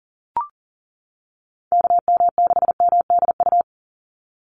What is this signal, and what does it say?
Morse code, heard as a keyed tone, about 40 wpm, 700 Hz. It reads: KM6MDU
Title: USA Call Signs 40wpm